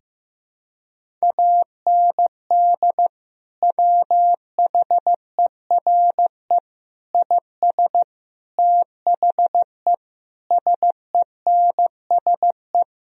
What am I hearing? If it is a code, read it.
Morse code, heard as a keyed tone, about 15 wpm, 700 Hz. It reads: AND WHERE IS THE SENSE
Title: Sentences from Top 1000 Words - Review Part 4 - 15wpm